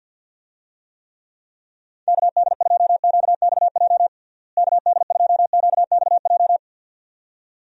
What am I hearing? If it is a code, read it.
Morse code, heard as a keyed tone, about 47 wpm, 700 Hz. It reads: KD2QXJ KD2QXJ
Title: USA Call Signs - 6 Characters - Repeated 2x - 50wpm